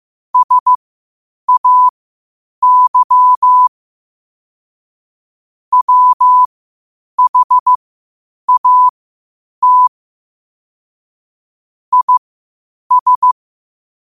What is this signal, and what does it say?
Morse code, heard as a keyed tone, about 15 wpm, spaced as for 8 wpm, 1000 Hz. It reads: SAY WHAT IS
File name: qotdfiles / 8wpm